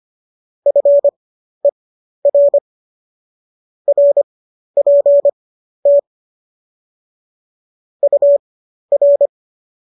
Morse code, heard as a keyed tone, about 25 wpm, spaced as for 12 wpm, 570 Hz. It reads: FER RPT UR
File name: qso009 12